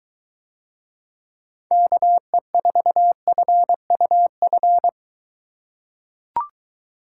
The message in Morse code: KE4FUF

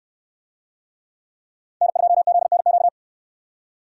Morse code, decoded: N2ZNQ

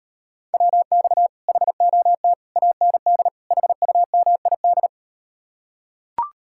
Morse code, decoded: WX HOT AND HUMID

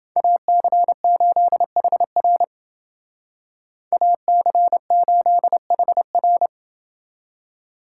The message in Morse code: AC8HR AC8HR

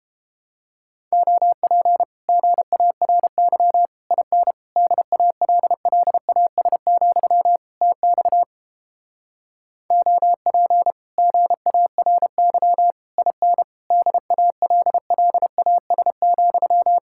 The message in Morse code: OP GARY IN DALLAS, TX OP GARY IN DALLAS,

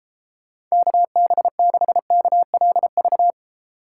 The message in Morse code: KB6KLV

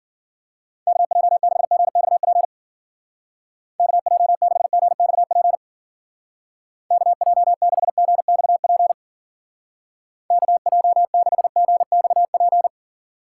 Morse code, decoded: KJ6GXP KJ6GXP KJ6GXP KJ6GXP

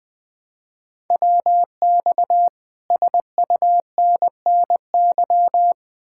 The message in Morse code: WX SUNNY

